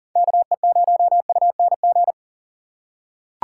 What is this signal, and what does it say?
Morse code, heard as a keyed tone, about 40 wpm, 700 Hz. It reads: KE0UNG
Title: USA Call Signs - 6 Characters - 2x Character Spacing - 40wpm